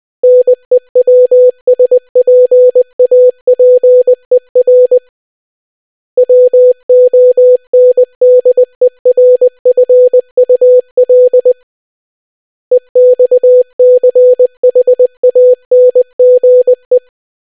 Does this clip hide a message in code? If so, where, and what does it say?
Yes: Morse code from 0.2 to 17.0 s, heard as a keyed tone, about 20 wpm, 500 Hz. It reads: NEWSPAPER WONDERFUL EXCHANGE